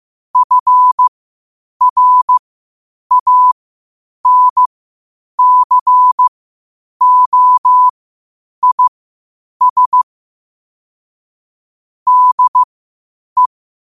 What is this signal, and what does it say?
Morse code, heard as a keyed tone, about 15 wpm, spaced as for 8 wpm, 1000 Hz. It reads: FRANCOIS DE